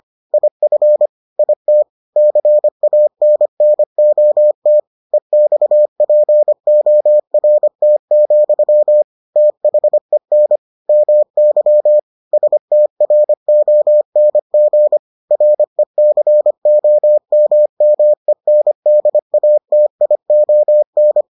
Morse code, heard as a keyed tone, about 25 wpm, 600 Hz. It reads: IF IT CANNOT EXPORT, THEN MY STRONG RECOMMENDATION